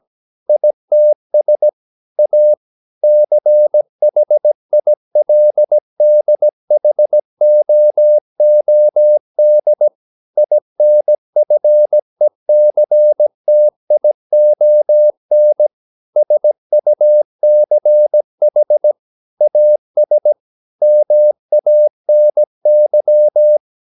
Morse code, heard as a keyed tone, about 17 wpm, 600 Hz. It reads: ITS A CHILDHOOD INFECTION SUCH AS MANY
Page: Romanen Never af Ken Follet på engelsk.